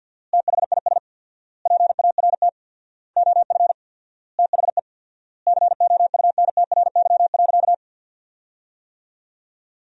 Morse code, heard as a keyed tone, about 48 wpm, 700 Hz. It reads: THIS PART OF THE COUNTRY.
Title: 4x Word-Spacing - Sentences from Top 300 Words - 50wpm